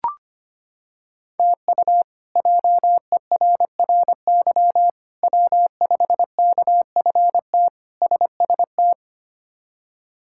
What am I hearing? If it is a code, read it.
Morse code, heard as a keyed tone, about 25 wpm, 700 Hz. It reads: TU JERRY W5KFT SST
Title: K1USN SST QSO Practice - 25wpm